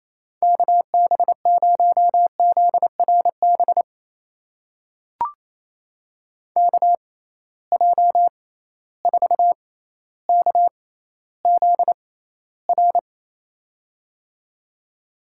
Morse code, heard as a keyed tone, about 28 wpm, spaced as for 14 wpm, 700 Hz. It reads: KB0ZRB KJ4KZR
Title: USA Call Signs - 6 Characters - 6x Character Spacing - 28wpm